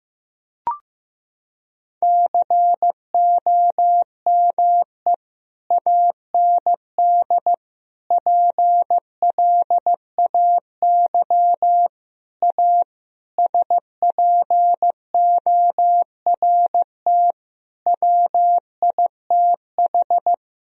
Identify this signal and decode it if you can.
Morse code, heard as a keyed tone, about 15 wpm, 700 Hz. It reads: COME AND PLAY A SPORT WITH
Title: Sentences from Top 1000 Words - Review Part 4 - 15wpm